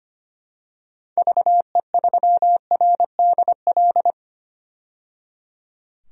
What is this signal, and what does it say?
Morse code, heard as a keyed tone, about 25 wpm, 700 Hz. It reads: VE3RDL
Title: DXCC-100 Call Signs - 6 Characters - 25wpm